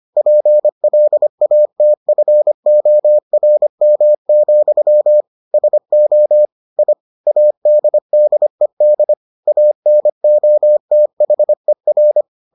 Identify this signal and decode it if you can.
Morse code, heard as a keyed tone, about 25 wpm, 600 Hz. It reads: PLATFORM, SO I ADDED ANOTHER